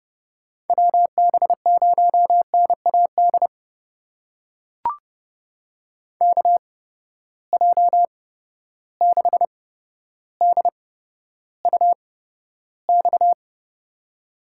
Morse code, heard as a keyed tone, about 31 wpm, spaced as for 13 wpm, 700 Hz. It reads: WB0NAD KJ6DUX
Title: USA Call Signs - 6 Characters - 8x Character Spacing - 30wpm